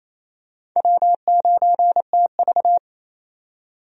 Morse code, heard as a keyed tone, about 28 wpm, 700 Hz. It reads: W9TV
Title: USA Call Signs - 4 Characters - 28wpm